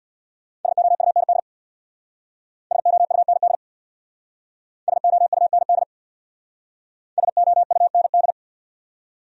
Morse code, heard as a keyed tone, about 62 wpm, 700 Hz. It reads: SOUND SOUND SOUND SOUND
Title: Speed-Racing Top 200 Words 50wpm